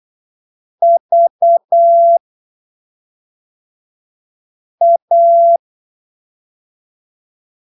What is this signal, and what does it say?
Morse code, heard as a keyed tone, about 8 wpm, spaced as for 3 wpm, 680 Hz. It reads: VA
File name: arquivos / texto / texto 13 08ppm